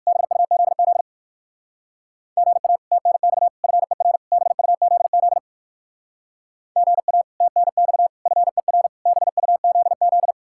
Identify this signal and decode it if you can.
Morse code, heard as a keyed tone, about 69 wpm, spaced as for 59 wpm, 700 Hz. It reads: BUZZ GA TNX FER BUZZ GA TNX FER BUZZ